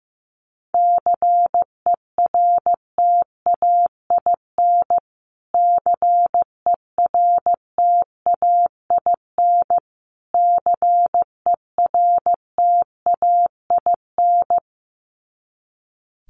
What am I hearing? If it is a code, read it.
Morse code, heard as a keyed tone, about 15 wpm, 700 Hz. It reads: CERTAIN CERTAIN CERTAIN
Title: Top 500 Words Repeated 15wpm